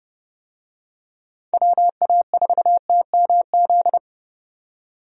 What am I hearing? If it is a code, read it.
Morse code, heard as a keyed tone, about 32 wpm, 700 Hz. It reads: WA4TMZ
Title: USA Call Signs - 6 Characters - 30wpm